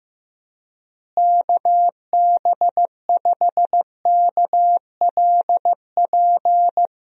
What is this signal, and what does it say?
Morse code, heard as a keyed tone, about 15 wpm, 700 Hz. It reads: KB5KLP
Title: USA Call Signs - 6 Characters - Repeated 2x - 15wpm